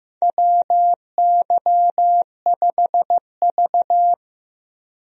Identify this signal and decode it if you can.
Morse code, heard as a keyed tone, about 15 wpm, 700 Hz. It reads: WY5V
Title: USA Call Signs - 4 Characters - 15wpm